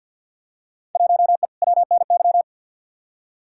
Morse code, heard as a keyed tone, about 47 wpm, 700 Hz. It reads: 1E WNY
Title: Field Day Exchanges - 50wpm